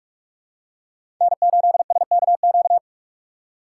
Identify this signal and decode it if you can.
Morse code, heard as a keyed tone, about 46 wpm, 700 Hz. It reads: N8SKQ
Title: USA Call Signs - 5 Characters - 6x Character Spacing - 45wpm